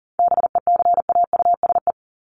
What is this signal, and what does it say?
Morse code, heard as a keyed tone, about 40 wpm, 700 Hz. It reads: BECAUSE
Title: Top 501-750 Words 40wpm